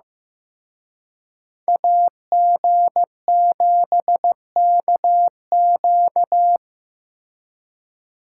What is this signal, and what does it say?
Morse code, heard as a keyed tone, about 18 wpm, 700 Hz. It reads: AG7KQ
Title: Speed Racing - USA 2x2 Call Signs 15wpm